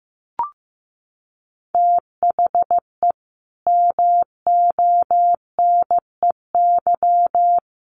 Morse code, heard as a keyed tone, about 14 wpm, 700 Hz. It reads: THE MONEY
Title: Sentences from Top 1400 Words 15wpm